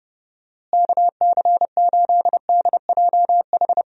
Morse code, heard as a keyed tone, about 31 wpm, 700 Hz. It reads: KC8DJH
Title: USA Call Signs - 6 Characters - Repeated 3x - 30wpm